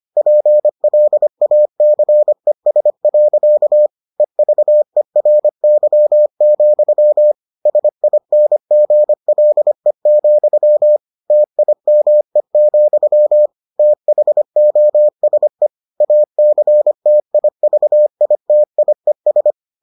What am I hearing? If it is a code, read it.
Morse code, heard as a keyed tone, about 25 wpm, 600 Hz. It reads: PLACES. EVERY, SINGLE, TIME, THOSE ACTIVITIES